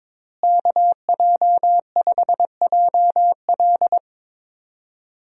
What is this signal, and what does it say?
Morse code, heard as a keyed tone, about 22 wpm, 700 Hz. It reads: KJ5JL
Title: USA Call Signs - 5 Characters - Repeated 3x - 22wpm